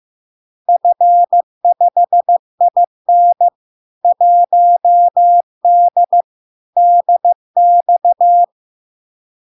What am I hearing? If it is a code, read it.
Morse code, heard as a keyed tone, about 15 wpm, 700 Hz. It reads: F5IN 1D DX
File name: LARK-FD-2022-Log-in-CW-15wpm-700-Hz